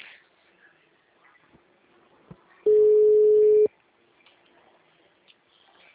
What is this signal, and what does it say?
Ringback tone (Europe, 425 Hz).
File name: out- -2015- - - .93826